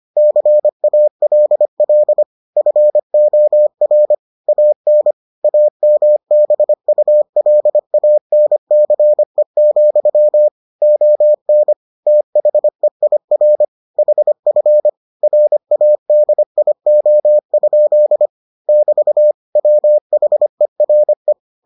Morse code, heard as a keyed tone, about 25 wpm, 600 Hz. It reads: CALL FOR AN AMBULANCE, ON THEIR HF RADIO? <BT> WHERE